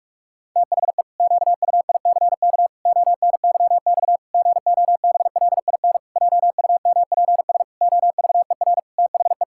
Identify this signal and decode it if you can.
Morse code, heard as a keyed tone, about 46 wpm, 700 Hz. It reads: THE QUICK ONYX GOBLIN JUMPS OVER THE
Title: Pangram Challenge v2 - 45wpm